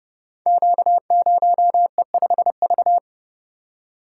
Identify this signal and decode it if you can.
Morse code, heard as a keyed tone, about 30 wpm, 700 Hz. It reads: Q0E5V
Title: Code Groups - 5 Letter:Numbers - 30wpm